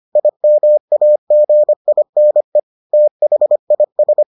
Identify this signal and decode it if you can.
Morse code, heard as a keyed tone, about 25 wpm, 600 Hz. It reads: IMAGINE THIS